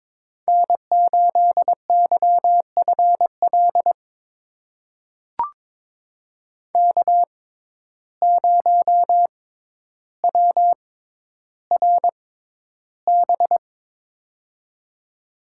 Morse code, heard as a keyed tone, about 21 wpm, spaced as for 11 wpm, 700 Hz. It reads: N8YFL K0WRB
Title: USA Call Signs - 5 Characters - 6x Character Spacing - 22wpm